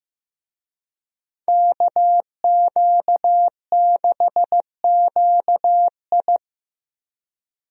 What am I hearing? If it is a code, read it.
Morse code, heard as a keyed tone, about 15 wpm, 700 Hz. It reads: KQ6QI
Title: USA Call Signs - 5 Characters - Repeated 2x - 15wpm